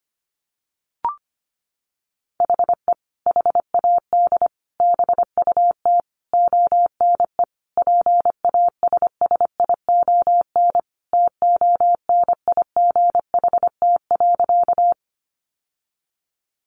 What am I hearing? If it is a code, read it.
Morse code, heard as a keyed tone, about 25 wpm, 700 Hz. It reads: HE HAD BUT ONE PASSION TONIGHT.
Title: Sentences from Top 1300 Words 25wpm